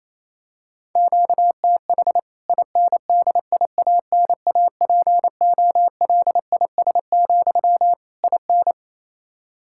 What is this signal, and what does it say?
Morse code, heard as a keyed tone, about 28 wpm, 700 Hz. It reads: QTH INDIANAPOLIS, IN
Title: K1USN SST QSO Practice 28wpm